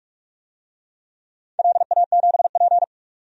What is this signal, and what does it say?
Morse code, heard as a keyed tone, about 47 wpm, 700 Hz. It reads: LA7P
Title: DXCC-100 Call Signs - 4 Characters - 45wpm